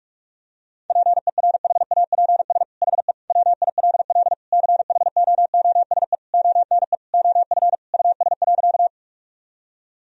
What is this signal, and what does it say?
Morse code, heard as a keyed tone, about 46 wpm, 700 Hz. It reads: PERHAPS HE WILL CHOOSE ONE OF US.
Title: Sentences from Top 1000 Words - Review Part 4 - 45wpm